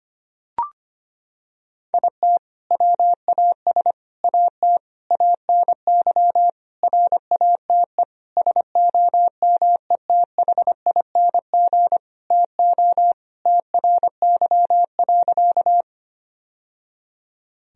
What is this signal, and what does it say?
Morse code, heard as a keyed tone, about 25 wpm, 700 Hz. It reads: IT WAS AT ANY RATE SOMETHING TO TRY.
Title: Sentences from Top 1000 Words - Review Part 4 - 25wpm